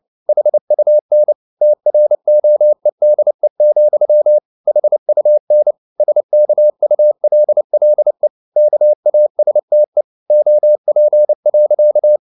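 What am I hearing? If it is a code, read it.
Morse code, heard as a keyed tone, about 29 wpm, 600 Hz. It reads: HUN TROEDE, HUN SKULLE KASTE OP.
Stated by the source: Fast (QRQ) Morse at 29 wpm, from CW med Gnister.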